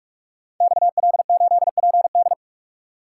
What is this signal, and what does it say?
Morse code, heard as a keyed tone, about 44 wpm, 700 Hz. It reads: XL8PD
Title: Code Groups - 5 Letter:Numbers - 45wpm